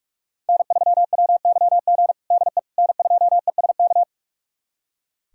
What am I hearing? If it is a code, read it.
Morse code, heard as a keyed tone, about 41 wpm, 700 Hz. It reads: N3WYG DE N2ESK
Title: USA Callsign DE Callsign - 5 Characters - 45wpm